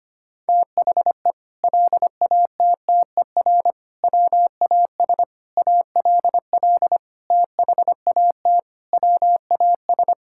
Morse code, heard as a keyed tone, about 25 wpm, 700 Hz. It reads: THE LATTER WAS ALL THAT WAS
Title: Sentences from Top 1600 Words 25wpm